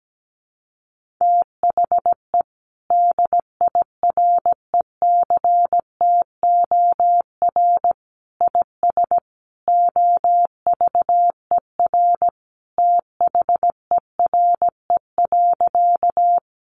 Morse code, heard as a keyed tone, about 17 wpm, 700 Hz. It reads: THE DIRECTOR IS OVER THERE.
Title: Sentences from Top 1300 Words 17wpm